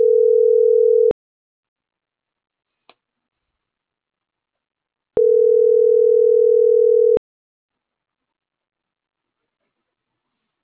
Ringback tone (North America), 440 + 480 Hz.